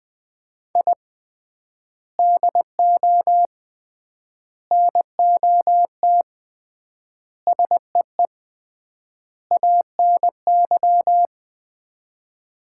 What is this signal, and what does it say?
Morse code, heard as a keyed tone, about 20 wpm, 690 Hz. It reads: I DO NOT SEE ANY
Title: 3x Word-Spacing - Sets of 5 Words - 20wpm